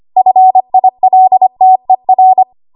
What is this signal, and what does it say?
Morse code, heard as a keyed tone, about 25 wpm, 750 Hz. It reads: FILTER
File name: Archive / 25 WPM / 25WPM